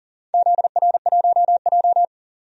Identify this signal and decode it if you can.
Morse code, heard as a keyed tone, about 38 wpm, 700 Hz. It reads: ZR1J
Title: DXCC-100 Call Signs - 4 Characters - 40wpm